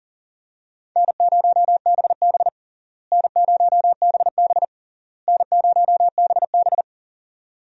Morse code, heard as a keyed tone, about 40 wpm, 700 Hz. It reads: N0BB N0BB N0BB
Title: USA Call Signs - 4 Characters - Repeated 3x - 40wpm